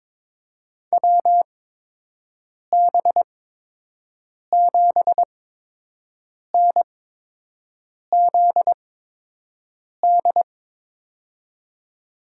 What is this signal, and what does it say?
Morse code, heard as a keyed tone, about 23 wpm, spaced as for 6 wpm, 700 Hz. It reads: WB7NZD
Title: USA Call Signs - 6 Characters - 8x Character Spacing - 22wpm